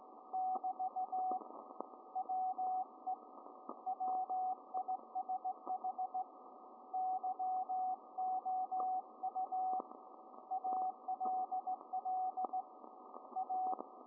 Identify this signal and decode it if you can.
Morse code, heard as a keyed tone, about 17 wpm, 750 Hz, with radio noise. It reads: <BT> WE WISH YOU ALL A